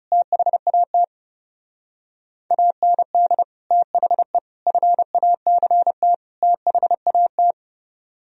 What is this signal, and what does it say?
Morse code, heard as a keyed tone, about 38 wpm, 700 Hz. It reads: THAT AND THE FACT THAT
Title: Speed-Racing - Sets of 4 Words 30wpm